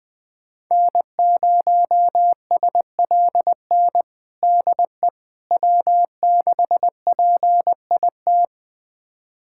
Morse code, heard as a keyed tone, about 20 wpm, 700 Hz. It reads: N0SLN DE W6PIT
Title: USA Callsign DE Callsign - 5 Characters - 20wpm